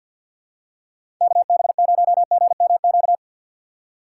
Morse code, heard as a keyed tone, about 52 wpm, 700 Hz. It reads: KB0GMQ
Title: USA Call Signs - 6 Characters - 4x Character Spacing - 50wpm